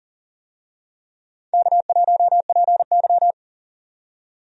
Morse code, heard as a keyed tone, about 40 wpm, 700 Hz. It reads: K1PY
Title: USA Call Signs - 4 Characters - 40wpm